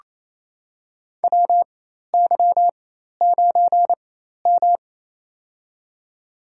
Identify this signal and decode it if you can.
Morse code, heard as a keyed tone, about 28 wpm, spaced as for 13 wpm, 700 Hz. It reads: WY9M
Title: USA Call Signs - 4 Characters - 4x Character Spacing - 28wpm